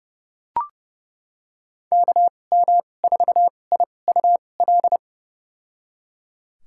Morse code, heard as a keyed tone, about 30 wpm, spaced as for 22 wpm, 700 Hz. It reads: KM4IUL
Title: USA Call Signs - 6 Characters - 2x Character Spacing - 30wpm